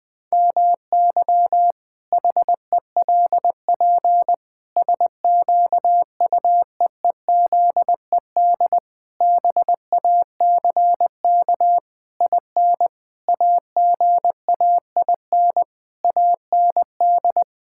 Morse code, heard as a keyed tone, about 20 wpm, 700 Hz. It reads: MY HELP SQUEEZED BACK IN AGAIN AND
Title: Pangram Challenge v2 - 20wpm